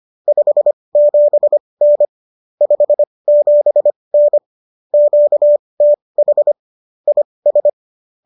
Morse code, heard as a keyed tone, about 25 wpm, spaced as for 20 wpm, 590 Hz. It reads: 57N 57N QTH IS